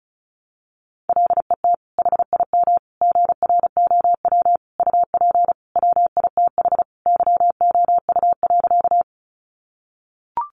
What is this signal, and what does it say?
Morse code, heard as a keyed tone, about 34 wpm, 700 Hz. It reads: LET HIM GROW UP WITH YOU.
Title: Sentences from Top 300 Words 35wpm